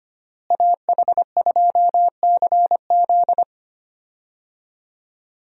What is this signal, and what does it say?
Morse code, heard as a keyed tone, about 25 wpm, 700 Hz. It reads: AH2CZ
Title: USA Call Signs - 5 Characters - Repeated 3x - 25wpm